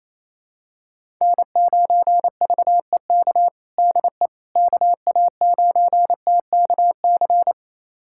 Morse code, heard as a keyed tone, about 28 wpm, 700 Hz. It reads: N9VEK DE KA9TKC
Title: USA Callsign DE Callsign 28wpm